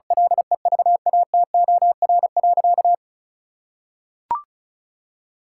Morse code, heard as a keyed tone, about 34 wpm, 700 Hz. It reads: LEVATOR.
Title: Sentences from Top 1600 Words 35wpm